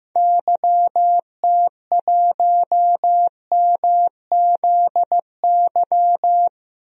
Morse code, heard as a keyed tone, about 15 wpm, 700 Hz. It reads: YT1MZY
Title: DXCC-100 Call Signs - 6 Characters - Repeated 2x - 15wpm